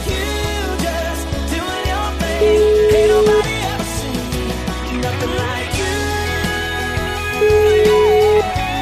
Ringback tone (Europe, 425 Hz), with noise.